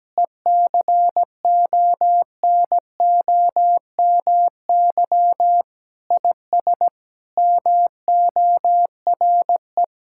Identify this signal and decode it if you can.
Morse code, heard as a keyed tone, about 17 wpm, 700 Hz. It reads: ECONOMY IS MORE
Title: Sentences from Top 1000 Words - Review Part 6 - 17wpm